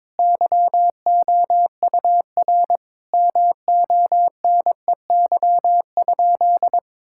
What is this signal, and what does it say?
Morse code, heard as a keyed tone, about 22 wpm, 700 Hz. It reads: YOUR MONEY?
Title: Sentences from Top 1000 Words - Review Part 5 - 22wpm